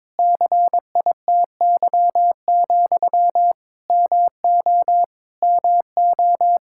Morse code, heard as a keyed tone, about 21 wpm, 700 Hz. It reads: CITY, MO MO
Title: Practice QSO - Exchange 1 - Beginning - 22wpm